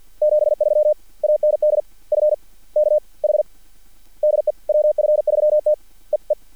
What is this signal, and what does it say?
Morse code, heard as a keyed tone, about 49 wpm, 600 Hz. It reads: 73 AND F K H DE WF3T E E
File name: archived / CRSnet